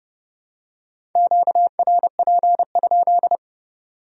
Morse code, heard as a keyed tone, about 30 wpm, 700 Hz. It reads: QRP?